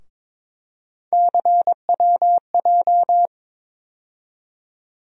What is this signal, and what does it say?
Morse code, heard as a keyed tone, about 21 wpm, 700 Hz. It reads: CWJ
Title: USA Call Sign Postfixes - 3 Letters 22wpm